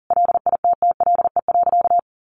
Morse code, heard as a keyed tone, about 40 wpm, 700 Hz. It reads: LITTLE.
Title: Sentences from Top 1500 Words 40wpm